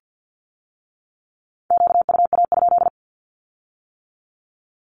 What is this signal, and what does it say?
Morse code, heard as a keyed tone, about 50 wpm, 700 Hz. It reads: QUA?
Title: Q-Codes with Spoken Meaning - 50wpm